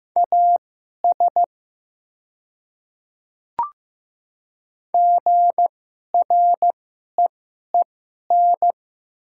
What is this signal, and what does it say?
Morse code, heard as a keyed tone, about 15 wpm, spaced as for 11 wpm, 700 Hz. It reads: AS GREEN
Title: 2x Character Spacing - Top 500 Words - 15wpm